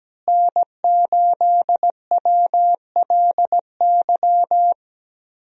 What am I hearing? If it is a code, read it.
Morse code, heard as a keyed tone, about 17 wpm, 700 Hz. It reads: N8WLY
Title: USA Call Signs - 5 Characters - Repeated 3x - 17wpm